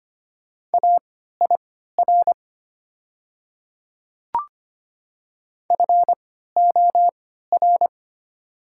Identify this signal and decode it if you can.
Morse code, heard as a keyed tone, about 25 wpm, spaced as for 14 wpm, 690 Hz. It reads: AIR FOR
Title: 3x Character Spacing - Top 200 Words - 25wpm